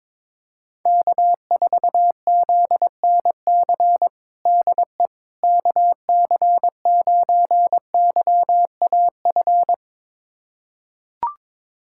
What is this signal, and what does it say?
Morse code, heard as a keyed tone, about 21 wpm, 700 Hz. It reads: K4ZNC DE KC9YAF
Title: USA Callsign DE Callsign 22wpm